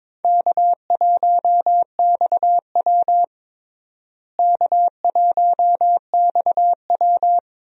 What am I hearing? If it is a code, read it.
Morse code, heard as a keyed tone, about 22 wpm, 700 Hz. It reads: K1XW K1XW
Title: USA Call Signs - 4 Characters - Repeated 2x - 22wpm